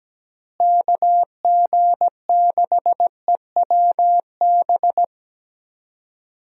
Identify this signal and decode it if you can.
Morse code, heard as a keyed tone, about 17 wpm, 700 Hz. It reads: KG6EWB
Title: USA Call Signs - 6 Characters - Repeated 3x - 17wpm